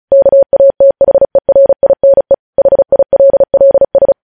Morse code, heard as a keyed tone, about 36 wpm, 550 Hz. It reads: KATHERINE HILLS
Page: news headlines in morse code at 35 wpm